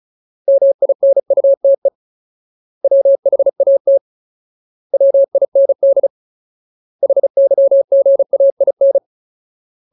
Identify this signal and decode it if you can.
Morse code, heard as a keyed tone, about 34 wpm, 550 Hz. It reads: MINUTE WHAT WIND HYGAIN
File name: Ultimate Word List / 35WPM Ultimate